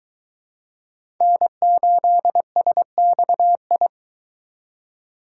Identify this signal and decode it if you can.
Morse code, heard as a keyed tone, about 24 wpm, 700 Hz. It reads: N8SXI